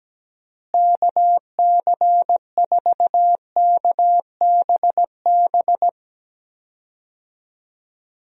Morse code, heard as a keyed tone, about 17 wpm, 700 Hz. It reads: KC4KBB
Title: USA Call Signs - 6 Characters - Repeated 3x - 17wpm